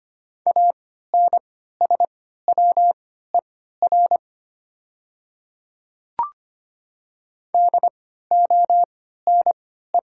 Morse code, heard as a keyed tone, about 25 wpm, spaced as for 14 wpm, 700 Hz. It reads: ANSWER DONE
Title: 3x Character Spacing - Top 500 Words - 25wpm